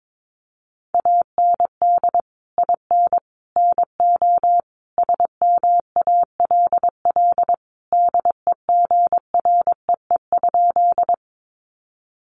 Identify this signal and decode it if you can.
Morse code, heard as a keyed tone, about 22 wpm, 700 Hz. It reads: AND IN NO SMALL DEGREE?
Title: Sentences from Top 1500 Words 22wpm